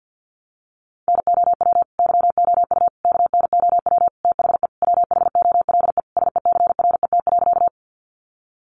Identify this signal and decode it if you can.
Morse code, heard as a keyed tone, about 49 wpm, 700 Hz. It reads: NOW YOU KNOW THE WHOLE SECRET.
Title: Sentences from Top 1200 Words 50wpm